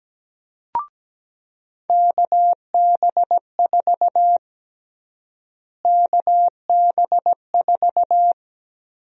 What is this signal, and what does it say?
Morse code, heard as a keyed tone, about 17 wpm, 700 Hz. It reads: KB4 KB4
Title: USA Call Sign Prefixes - Repeated 2x - 17wpm